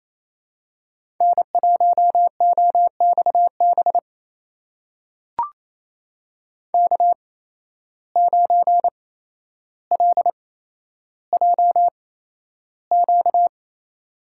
Morse code, heard as a keyed tone, about 28 wpm, spaced as for 12 wpm, 700 Hz. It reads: N1OXB K9LJQ
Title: USA Call Signs - 5 Characters - 8x Character Spacing - 28wpm